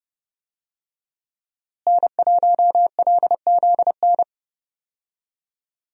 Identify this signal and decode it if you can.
Morse code, heard as a keyed tone, about 30 wpm, 700 Hz. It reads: N1LZN
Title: USA Call Signs - 5 Characters - Repeated 3x - 30wpm